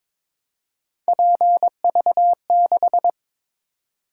Morse code, heard as a keyed tone, about 22 wpm, 700 Hz. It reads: PV6
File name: practice / 22wpm / Three Letter-Number Combinations - 22wpm